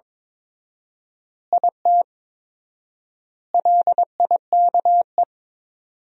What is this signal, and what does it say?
Morse code, heard as a keyed tone, about 21 wpm, 700 Hz. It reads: IT LIKE
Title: 4x Word-Spacing - Sentences from Top 300 Words - 22wpm